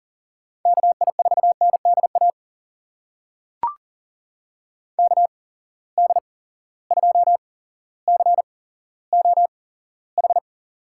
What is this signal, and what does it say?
Morse code, heard as a keyed tone, about 40 wpm, spaced as for 17 wpm, 700 Hz. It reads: KI4NDA KD2COH
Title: USA Call Signs - 6 Characters - 8x Character Spacing - 40wpm